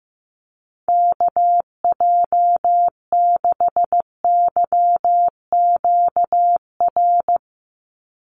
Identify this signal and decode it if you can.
Morse code, heard as a keyed tone, about 15 wpm, 700 Hz. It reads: KJ6YQR
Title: USA Call Signs 15wpm